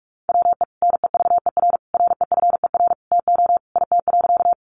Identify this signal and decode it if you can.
Morse code, heard as a keyed tone, about 44 wpm, 700 Hz. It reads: WE NEVER REFER TO IT.
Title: Sentences from Top 1000 Words 45wpm